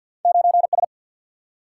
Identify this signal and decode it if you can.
Morse code, heard as a keyed tone, about 50 wpm, 700 Hz. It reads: 9 S
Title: Single-Letter Single-Number 50wpm